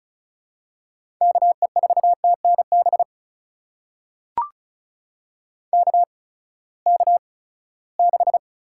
Morse code, heard as a keyed tone, about 34 wpm, spaced as for 20 wpm, 700 Hz. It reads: KE4TNB KK6
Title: USA Call Signs - 6 Characters - 8x Character Spacing - 35wpm